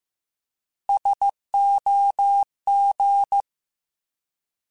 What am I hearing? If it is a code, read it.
Morse code, heard as a keyed tone, about 15 wpm, 800 Hz. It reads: SOG
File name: Alphabetical / Alphabetical mp3s / Ss @13